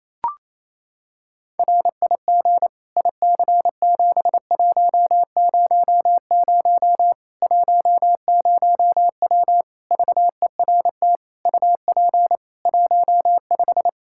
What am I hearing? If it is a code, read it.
Morse code, heard as a keyed tone, about 28 wpm, 700 Hz. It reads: RIG IC7100 10W VERT UP 15
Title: Practice QSO - Exchange 2 - Info - 28wpm